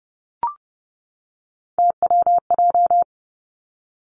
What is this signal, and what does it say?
Morse code, heard as a keyed tone, about 30 wpm, 700 Hz. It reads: TWJ
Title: Three Challenging Letter-Number Combinations 30wpm